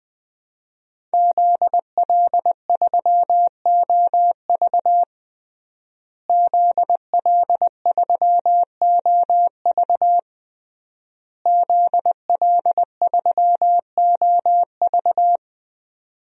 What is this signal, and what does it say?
Morse code, heard as a keyed tone, about 20 wpm, 700 Hz. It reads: ZL3OV ZL3OV ZL3OV
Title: DXCC-100 Call Signs - 5 Characters - Repeated 3x - 20wpm